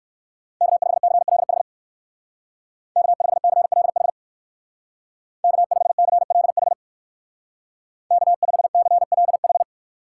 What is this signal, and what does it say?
Morse code, heard as a keyed tone, about 56 wpm, 700 Hz. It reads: K5CLH K5CLH K5CLH K5CLH
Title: Speed Racing - USA 1x3 Call Signs 45wpm